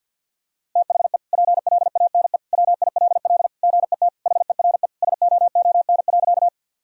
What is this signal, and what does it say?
Morse code, heard as a keyed tone, about 50 wpm, 700 Hz. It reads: THE PLANE WILL GET HERE SOON.
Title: Sentences from Top 1000 Words - Review Part 3 - 50wpm